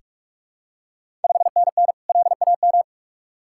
Morse code, heard as a keyed tone, about 44 wpm, 690 Hz. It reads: 5NN LAM
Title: Ontario QSO Party Exchange 45wpm